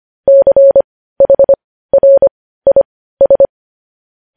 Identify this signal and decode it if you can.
Morse code, heard as a keyed tone, about 25 wpm, spaced as for 15 wpm, 550 Hz. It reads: CHRIS
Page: news-headlines-in-morse-code-at-15-wpm